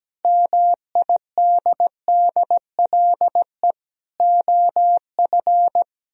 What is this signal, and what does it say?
Morse code, heard as a keyed tone, about 17 wpm, 700 Hz. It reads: MIDDLE OF
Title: Speed-Racing - Sets of 4 Words 17wpm